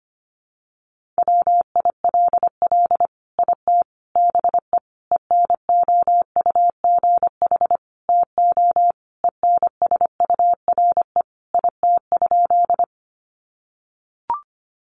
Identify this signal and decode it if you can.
Morse code, heard as a keyed tone, about 25 wpm, 690 Hz. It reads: WILL IT BE ENOUGH TO ENSURE IT?
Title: Sentences from Top 1100 Words 25wpm